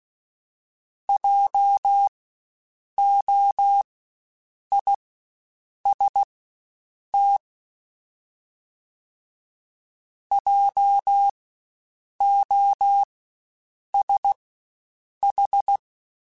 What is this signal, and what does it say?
Morse code, heard as a keyed tone, about 16 wpm, spaced as for 7 wpm, 800 Hz. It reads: JOIST JOSH
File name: Spaced Alphas / Spaced Alphas mp3s / Js sp @15